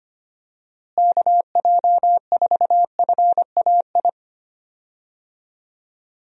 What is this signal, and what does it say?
Morse code, heard as a keyed tone, about 25 wpm, 700 Hz. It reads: KJ4FAI
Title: USA Call Signs - 6 Characters - Repeated 3x - 25wpm